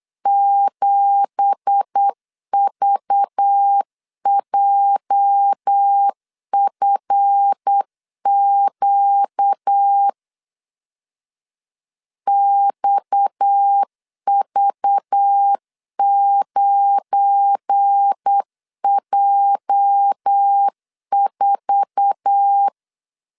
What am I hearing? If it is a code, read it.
Morse code, heard as a keyed tone, about 8 wpm, 800 Hz. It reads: 7VJFQ XV9J4
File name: 8wpm